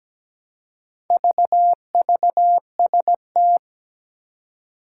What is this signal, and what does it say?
Morse code, heard as a keyed tone, about 17 wpm, 700 Hz. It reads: VVST
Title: Four Challenging Letter-Number Combinations 17wpm